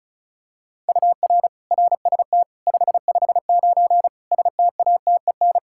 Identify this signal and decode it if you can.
Morse code, heard as a keyed tone, about 34 wpm, 700 Hz. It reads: UR RST 559 STATEN
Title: Practice QSO - Exchange 1 - Beginning - 35wpm